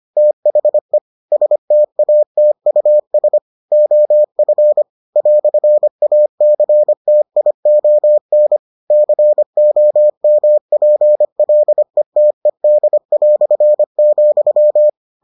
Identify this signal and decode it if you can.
Morse code, heard as a keyed tone, about 25 wpm, 600 Hz. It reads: THE STATUS OF "ACTION COMPLETED",